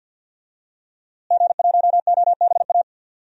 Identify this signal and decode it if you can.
Morse code, heard as a keyed tone, about 52 wpm, 700 Hz. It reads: G1OBA
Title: DXCC-100 Call Signs - 5 Characters - Repeated 3x - 50wpm